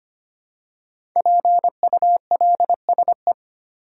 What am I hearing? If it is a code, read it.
Morse code, heard as a keyed tone, about 25 wpm, 700 Hz. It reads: PULSE
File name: 5 Letter Words - First 3 Letters Given - 25wpm